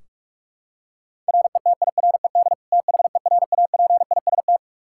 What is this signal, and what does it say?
Morse code, heard as a keyed tone, about 42 wpm, 700 Hz. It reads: RETIRED THERAPIST
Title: Practice QSO - Exchange 3 - More Info - 45wpm